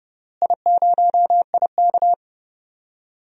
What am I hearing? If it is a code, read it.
Morse code, heard as a keyed tone, about 30 wpm, 700 Hz. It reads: I0IK
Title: DXCC-100 Call Signs - 4 Characters - Repeated 3x - 30wpm